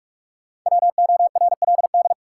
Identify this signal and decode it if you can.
Morse code, heard as a keyed tone, about 44 wpm, 700 Hz. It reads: WORLD
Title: Sets of 2 Words - New Word Drill - 45wpm